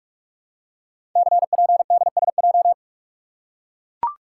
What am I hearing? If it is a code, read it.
Morse code, heard as a keyed tone, about 44 wpm, 700 Hz. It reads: CPDSJ
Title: Code Groups - 5 Letter:Numbers - 45wpm